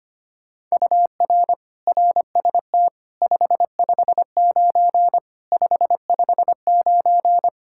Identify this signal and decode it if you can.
Morse code, heard as a keyed tone, about 25 wpm, 700 Hz. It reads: UR RST 559 559
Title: Practice QSO - Exchange 1 - Beginning - 25wpm